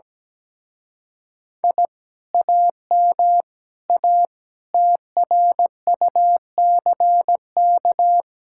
Morse code, heard as a keyed tone, about 17 wpm, 700 Hz. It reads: I AM A TRUCK
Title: Speed-Racing Common QSO Phrases Encore - 17wpm